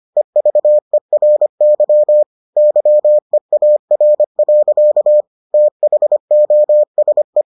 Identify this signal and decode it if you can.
Morse code, heard as a keyed tone, about 25 wpm, 600 Hz. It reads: EVERY YEAR. THOSE